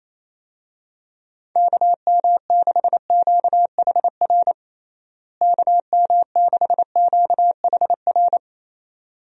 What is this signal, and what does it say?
Morse code, heard as a keyed tone, about 28 wpm, 700 Hz. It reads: KM6QHR KM6QHR
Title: USA Call Signs - 6 Characters - Repeated 2x - 28wpm